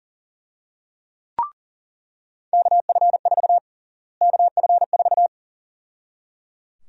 Morse code, heard as a keyed tone, about 37 wpm, 700 Hz. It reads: KF4 KF4
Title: USA Call Sign Prefixes - Repeated 2x - 40wpm